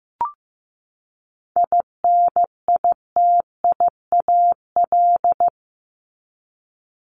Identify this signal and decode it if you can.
Morse code, heard as a keyed tone, about 15 wpm, 700 Hz. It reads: INITIAL
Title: Top 1501-1750 Words 15wpm